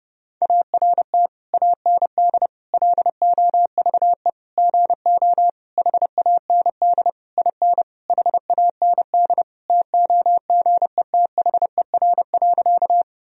Morse code, heard as a keyed tone, about 32 wpm, 700 Hz. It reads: ART AND LOVE GO HAND IN HAND TOGETHER.
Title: Sentences from Top 1000 Words - Review Part 3 - 30wpm